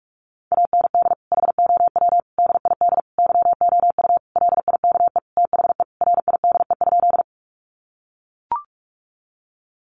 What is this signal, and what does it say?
Morse code, heard as a keyed tone, about 44 wpm, 700 Hz. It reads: AND HOW DID YOU LIKE THE RIDE?
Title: Sentences from Top 1200 Words 45wpm